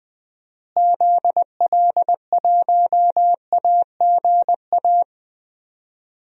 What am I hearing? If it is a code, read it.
Morse code, heard as a keyed tone, about 21 wpm, 700 Hz. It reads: ZL1AGA
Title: DXCC-100 Call Signs - 6 Characters - Repeated 3x - 20wpm